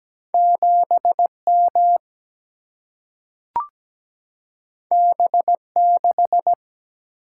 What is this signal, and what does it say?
Morse code, heard as a keyed tone, about 17 wpm, 700 Hz. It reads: 7M B6
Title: Single-Letter Single-Number 17wpm